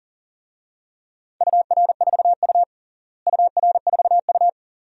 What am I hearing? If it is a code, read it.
Morse code, heard as a keyed tone, about 40 wpm, 700 Hz. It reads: UR4U UR4U
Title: DXCC-100 Call Signs - 4 Characters - Repeated 2x - 40wpm